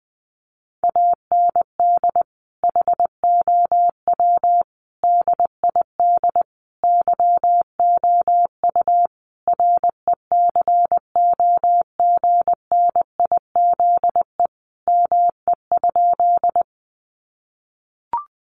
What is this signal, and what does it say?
Morse code, heard as a keyed tone, about 20 wpm, 700 Hz. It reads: AND HOW DID YOU RECOGNIZE ME?
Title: Sentences from Top 1400 Words 20wpm